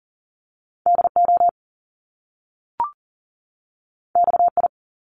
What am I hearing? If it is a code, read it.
Morse code, heard as a keyed tone, about 40 wpm, 700 Hz. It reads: DO XI
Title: Top Two Letter Combinations in Words 40wpm